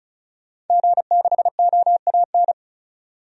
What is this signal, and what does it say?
Morse code, heard as a keyed tone, about 36 wpm, 700 Hz. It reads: G6OAN
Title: DXCC-100 Call Signs - 5 Characters - 35wpm